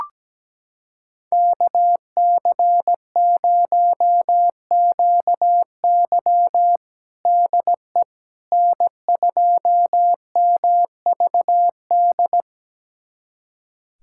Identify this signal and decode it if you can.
Morse code, heard as a keyed tone, about 17 wpm, 700 Hz. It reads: KC0QY DE N2MVD
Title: USA Callsign DE Callsign - 5 Characters - 17wpm